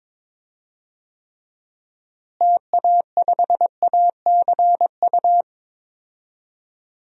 Morse code, heard as a keyed tone, about 22 wpm, 700 Hz. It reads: TA5ACU
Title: DXCC-100 Call Signs - 6 Characters - Repeated 2x - 22wpm